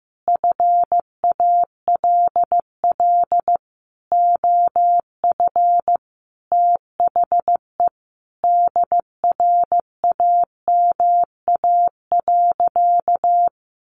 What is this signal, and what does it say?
Morse code, heard as a keyed tone, about 15 wpm, 700 Hz. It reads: FALL OF THE DRAMA.
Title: Sentences from Top 1500 Words 15wpm